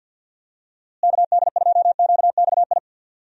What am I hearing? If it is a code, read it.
Morse code, heard as a keyed tone, about 52 wpm, 700 Hz. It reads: KD2QXI
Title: USA Call Signs - 6 Characters - Repeated 3x - 50wpm